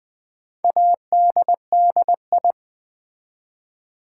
Morse code, heard as a keyed tone, about 20 wpm, 700 Hz. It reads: ADDI
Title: Build A Word - 4 Letters At A Time - 12-Letter Words - 20wpm